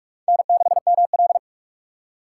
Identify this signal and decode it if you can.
Morse code, heard as a keyed tone, about 44 wpm, 700 Hz. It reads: N6ML